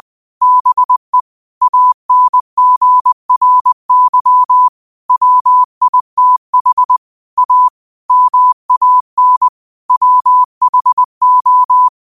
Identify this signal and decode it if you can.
Morse code, heard as a keyed tone, about 20 wpm, 1000 Hz. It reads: BE ANGRY WITH A MAN WHO
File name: qotdfiles / 20wpm